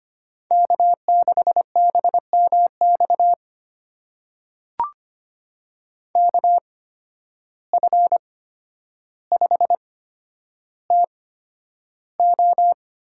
Morse code, heard as a keyed tone, about 25 wpm, spaced as for 11 wpm, 700 Hz. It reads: K6BMX KF5TO
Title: USA Call Signs - 5 Characters - 8x Character Spacing - 25wpm